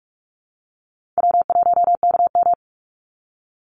Morse code, heard as a keyed tone, about 44 wpm, 700 Hz. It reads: W1KM
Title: Call Signs Repeated 3x - 45wpm